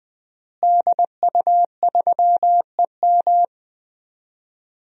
Morse code, heard as a keyed tone, about 20 wpm, 700 Hz. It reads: DU3EM
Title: DXCC-100 Call Signs - 5 Characters - Repeated 3x - 20wpm